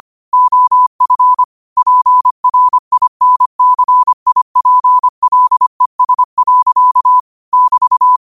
Morse code, heard as a keyed tone, about 25 wpm, 1000 Hz. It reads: OF PRINCIPLES. <BT>
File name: qotdfiles / 25wpm